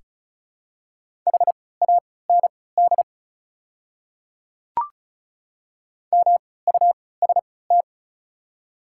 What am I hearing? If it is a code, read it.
Morse code, heard as a keyed tone, about 36 wpm, spaced as for 20 wpm, 700 Hz. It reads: HAND MUST
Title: 3x Character Spacing - Top 200 Words - 35wpm